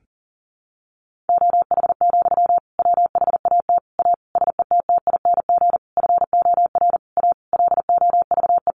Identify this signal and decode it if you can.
Morse code, heard as a keyed tone, about 40 wpm, 700 Hz. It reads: OH, WHAT A SETTING FOR A LOVE
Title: Sentences from Top 1300 Words 40wpm